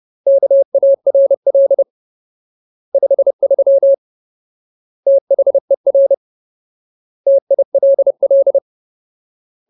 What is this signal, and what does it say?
Morse code, heard as a keyed tone, about 30 wpm, 550 Hz. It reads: KARL 53 THER TILL
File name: Ultimate Word List / 30WPM Ultimate